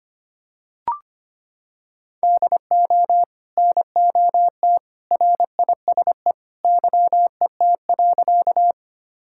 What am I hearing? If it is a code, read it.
Morse code, heard as a keyed tone, about 25 wpm, 700 Hz. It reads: DO NOT RISE YET.
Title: Sentences from Top 1000 Words - Review Part 6 - 25wpm